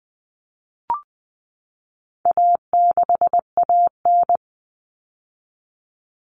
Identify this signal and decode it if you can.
Morse code, heard as a keyed tone, about 19 wpm, 700 Hz. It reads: A6AN
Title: Common Cut Numbers - 4-Digit 20wpm